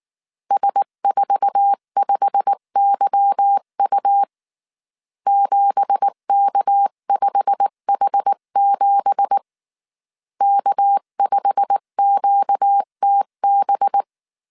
Morse code, heard as a keyed tone, about 19 wpm, 800 Hz. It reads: S45YU 7K5H7 K5QTB